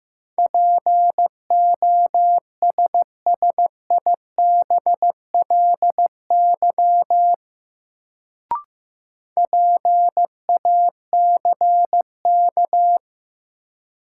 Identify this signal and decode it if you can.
Morse code, heard as a keyed tone, about 15 wpm, 700 Hz. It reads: POSSIBLY PACK
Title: Sentences from Top 1300 Words - New Word Drill Alt - 15wpm